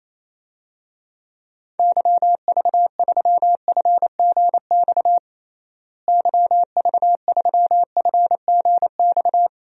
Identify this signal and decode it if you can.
Morse code, heard as a keyed tone, about 28 wpm, 700 Hz. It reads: YV3FGX YV3FGX
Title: DXCC-100 Call Signs - 6 Characters - Repeated 2x - 28wpm